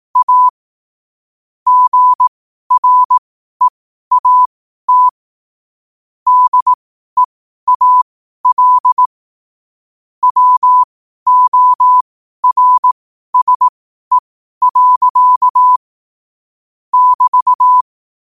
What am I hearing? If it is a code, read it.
Morse code, heard as a keyed tone, about 18 wpm, spaced as for 12 wpm, 1000 Hz. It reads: A GREAT DEAL WORSE. <BT>
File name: qotdfiles / 12wpm